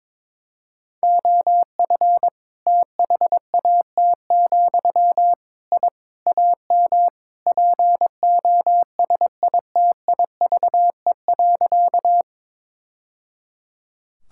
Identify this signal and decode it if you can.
Morse code, heard as a keyed tone, about 21 wpm, 700 Hz. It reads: OF THAT, I AM POSITIVE.
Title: Sentences from Top 1000 Words - Review Part 5 - 22wpm